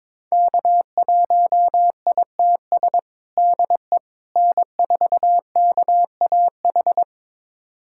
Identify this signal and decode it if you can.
Morse code, heard as a keyed tone, about 22 wpm, 700 Hz. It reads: K1ITS DE N4KAH
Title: USA Callsign DE Callsign - 5 Characters - 22wpm